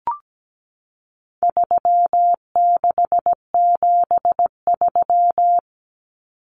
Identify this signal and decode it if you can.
Morse code, heard as a keyed tone, about 17 wpm, 700 Hz. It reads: 3673
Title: Common Cut Numbers - 4-Digit 17wpm